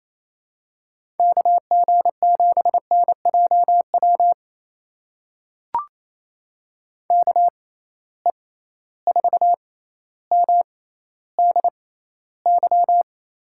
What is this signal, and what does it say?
Morse code, heard as a keyed tone, about 28 wpm, spaced as for 14 wpm, 700 Hz. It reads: KG7NJW KE4MDY
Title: USA Call Signs - 6 Characters - 6x Character Spacing - 28wpm